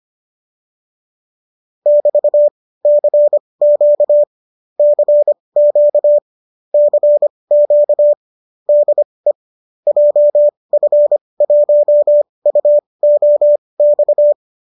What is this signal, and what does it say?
Morse code, heard as a keyed tone, about 25 wpm, spaced as for 19 wpm, 590 Hz. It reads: <BT>CQ CQ CQ DE JF1UOX